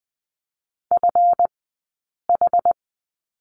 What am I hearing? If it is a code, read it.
Morse code, heard as a keyed tone, about 20 wpm, spaced as for 8 wpm, 700 Hz. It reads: FH
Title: 158 Lesson 26d - Letter g - Two Characters 20wpm